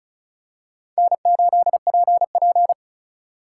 Morse code, heard as a keyed tone, about 34 wpm, 700 Hz. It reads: N8PP
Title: USA Call Signs - 3x Character Spacing - 35wpm